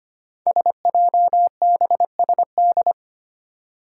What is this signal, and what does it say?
Morse code, heard as a keyed tone, about 25 wpm, 700 Hz. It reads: SJBSD
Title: Five Challenging Letter-Number Combinations 25wpm